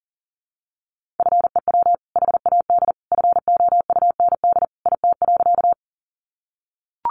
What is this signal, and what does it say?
Morse code, heard as a keyed tone, about 37 wpm, 700 Hz. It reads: FEW HAD FOUND IT.
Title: Sentences from Top 300 Words 40wpm